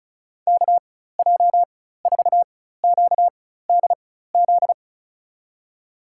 Morse code, heard as a keyed tone, about 36 wpm, spaced as for 16 wpm, 700 Hz. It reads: KJ4QDZ
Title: USA Call Signs - 6 Characters - 4x Character Spacing - 35wpm